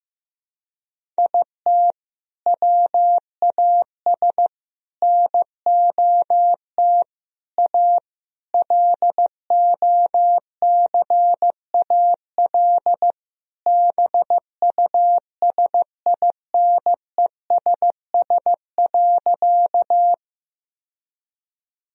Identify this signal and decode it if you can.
Morse code, heard as a keyed tone, about 16 wpm, 700 Hz. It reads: IT WAS NOT A LOCAL BUSINESS.
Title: Sentences from Top 1000 Words - Review Part 3 - 15wpm